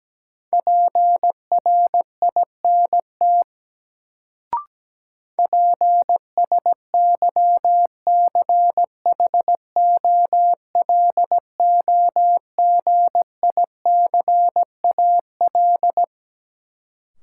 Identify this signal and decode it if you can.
Morse code, heard as a keyed tone, about 17 wpm, 700 Hz. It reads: PRINT PSYCHOLOGICAL
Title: Sentences from Top 1400 Words - New Word Drill - 17wpm